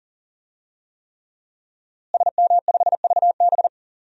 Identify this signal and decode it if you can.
Morse code, heard as a keyed tone, about 40 wpm, 700 Hz. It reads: SM5VB
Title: DXCC-100 Call Signs - 5 Characters - Repeated 3x - 40wpm